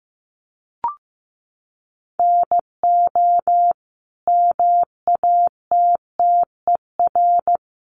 Morse code, heard as a keyed tone, about 15 wpm, 700 Hz. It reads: NO MATTER
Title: Sentences from Top 1200 Words 15wpm